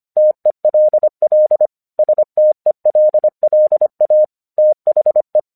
Morse code, heard as a keyed tone, about 24 wpm, 610 Hz. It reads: TELL STELLA THE
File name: 1 English 25wpm / AQC28